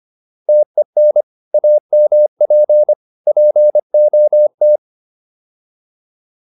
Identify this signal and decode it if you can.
Morse code, heard as a keyed tone, about 25 wpm, 600 Hz. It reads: TEN AMP POT